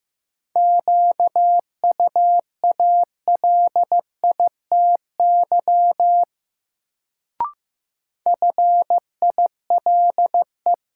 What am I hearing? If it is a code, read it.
Morse code, heard as a keyed tone, about 15 wpm, 700 Hz. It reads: QUALITY FILE
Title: Sentences from Top 1000 Words - Review Part 4 - New Word Drill - 15wpm